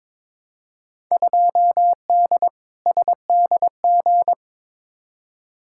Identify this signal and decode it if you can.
Morse code, heard as a keyed tone, about 22 wpm, 700 Hz. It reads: 2D SDG
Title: Field Day Exchanges Encore - 22wpm